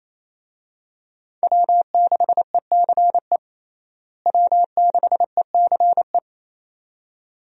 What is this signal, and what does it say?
Morse code, heard as a keyed tone, about 28 wpm, 700 Hz. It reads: W6ECE W6ECE
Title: USA Call Signs - 5 Characters - Repeated 2x - 28wpm